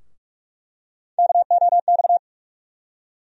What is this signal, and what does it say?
Morse code, heard as a keyed tone, about 47 wpm, 700 Hz. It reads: KOX
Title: USA Call Sign Postfixes - 3 Letters 45wpm